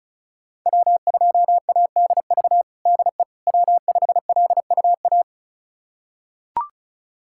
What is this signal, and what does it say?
Morse code, heard as a keyed tone, about 34 wpm, 700 Hz. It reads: W2ADV DE W5LUA
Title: USA Callsign DE Callsign - 5 Characters - 35wpm